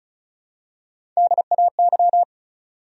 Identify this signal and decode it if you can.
Morse code, heard as a keyed tone, about 36 wpm, 700 Hz. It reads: DAY
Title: Top 100 Words 35wpm